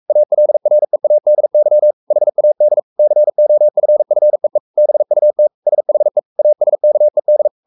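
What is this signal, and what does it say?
Morse code, heard as a keyed tone, about 44 wpm, 600 Hz. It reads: ALREADY HAD COFFEE BUT SHE ASKED
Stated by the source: Morse code at 43 wpm sending Ken Follett's novel Never in English, from CW med Gnister.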